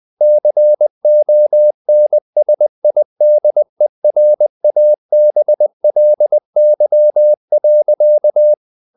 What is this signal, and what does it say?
Morse code, heard as a keyed tone, about 20 wpm, 600 Hz. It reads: CONSIDERABLY.
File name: the old man and the sea part 2 20wpm